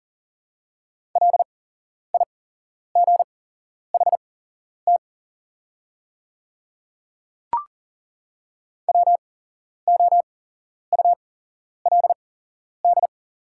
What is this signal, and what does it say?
Morse code, heard as a keyed tone, about 40 wpm, spaced as for 11 wpm, 690 Hz. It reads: LIGHT WOULD
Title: 8x Character Spacing - Top 200 Words - 40wpm